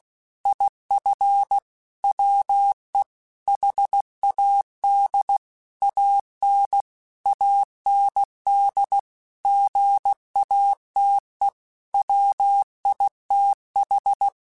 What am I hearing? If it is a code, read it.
Morse code, heard as a keyed tone, about 16 wpm, 800 Hz. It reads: IF WE HAD AN AND GATE WITH
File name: Technical Subjects / Technical Subjects-mp3s / AND Gate @15